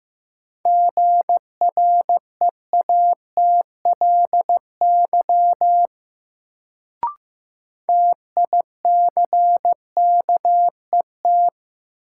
Morse code, heard as a keyed tone, about 16 wpm, 700 Hz. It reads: GREATLY TICKET
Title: Sentences from Top 1400 Words - New Word Drill Alt - 15wpm